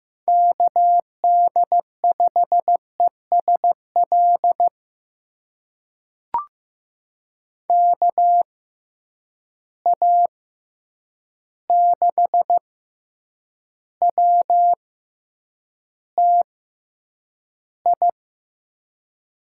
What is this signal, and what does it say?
Morse code, heard as a keyed tone, about 15 wpm, spaced as for 8 wpm, 700 Hz. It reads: KD5ESL KA6WTI
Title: USA Call Signs - 6 Characters - 6x Character Spacing - 15wpm